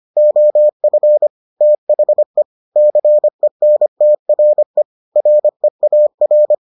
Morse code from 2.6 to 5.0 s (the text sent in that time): CENTRE